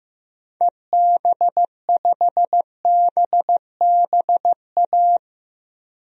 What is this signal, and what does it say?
Morse code, heard as a keyed tone, about 15 wpm, 700 Hz. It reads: EB5BBA
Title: DXCC-100 Call Signs - 6 Characters - Repeated 3x - 15wpm